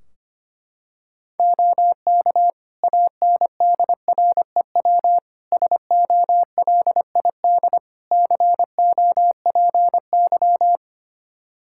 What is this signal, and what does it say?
Morse code, heard as a keyed tone, about 25 wpm, 700 Hz. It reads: OK ANDREW SOLID COPY
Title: Practice QSO - Exchange 3 - More Info - 25wpm